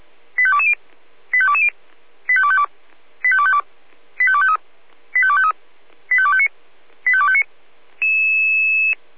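Five-tone selective call (ZVEI), heard as quick runs of five tones, tones of about 70 ms. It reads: 85220 85220 85252 85252 85253 85253 85259 85259
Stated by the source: ZVEI five-tone selective call 85218, received by SDR.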